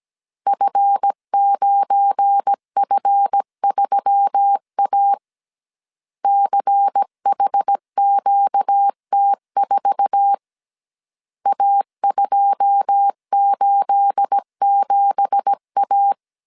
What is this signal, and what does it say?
Morse code, heard as a keyed tone, about 17 wpm, 800 Hz. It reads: F9F3A CHQT4 A287A
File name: CW / DET / 1 9 1500C S A 16wpm 2828